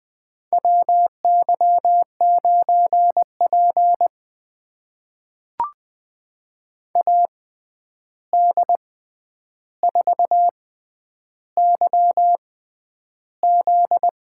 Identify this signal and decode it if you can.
Morse code, heard as a keyed tone, about 20 wpm, spaced as for 10 wpm, 700 Hz. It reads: WY9P AD4YZ
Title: USA Call Signs - 6x Character Spacing - 20wpm